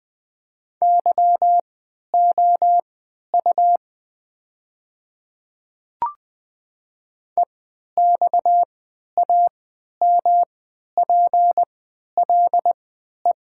Morse code, heard as a keyed tone, about 20 wpm, spaced as for 11 wpm, 700 Hz. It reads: YOU EXAMPLE
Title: 3x Character Spacing - Top 300 Words - 20wpm